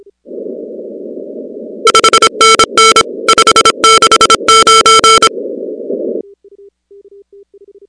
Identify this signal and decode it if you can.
Morse code, heard as a keyed tone, about 26 wpm, 1340 Hz, with noise. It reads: 5NN 569
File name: contests / 2019-cw-wpx-kt5j-k5pi- -2330 0-16